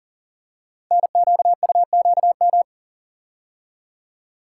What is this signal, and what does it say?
Morse code, heard as a keyed tone, about 38 wpm, 690 Hz. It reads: NQUQM
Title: Code Groups - 5 Letters - 40wpm